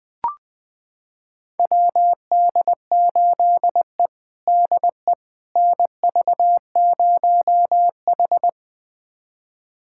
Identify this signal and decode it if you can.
Morse code, heard as a keyed tone, about 21 wpm, 700 Hz. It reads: WD8E DE NV0H
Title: USA Callsign DE Callsign - 4 Characters - 20wpm